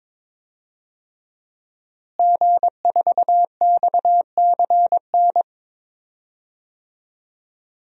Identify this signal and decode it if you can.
Morse code, heard as a keyed tone, about 22 wpm, 700 Hz. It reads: G4XCN
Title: DXCC-100 Call Signs - 5 Characters - Repeated 2x - 22wpm